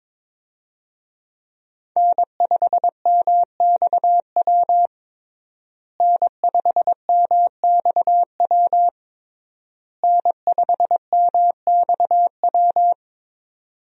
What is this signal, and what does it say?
Morse code, heard as a keyed tone, about 22 wpm, 700 Hz. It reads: N5MXW N5MXW N5MXW
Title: USA Call Signs - 5 Characters - Repeated 3x - 22wpm